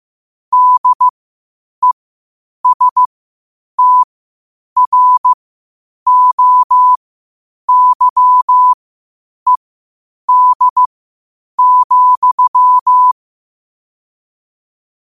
Morse code, heard as a keyed tone, about 15 wpm, spaced as for 8 wpm, 1000 Hz. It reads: DESTROYED,
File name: qotdfiles / 8wpm